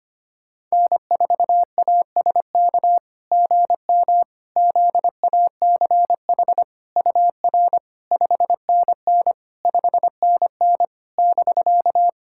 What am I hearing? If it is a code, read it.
Morse code, heard as a keyed tone, about 25 wpm, 690 Hz. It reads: N4ASK GM ZACH UR 5NN 5NN <BK>
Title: SOTA QSO Practice 25wpm